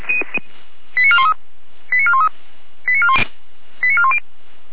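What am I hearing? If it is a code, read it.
Five-tone selective call (ZVEI), heard as quick runs of five tones, tones of about 70 ms. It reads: 89413 89413 89410
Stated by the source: ZVEI five-tone selective call 88097 received with an SDR.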